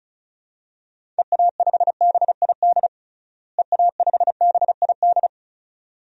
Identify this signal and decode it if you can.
Morse code, heard as a keyed tone, about 36 wpm, 700 Hz. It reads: EA5BID EA5BID
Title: DXCC-100 Call Signs - 6 Characters - Repeated 2x - 35wpm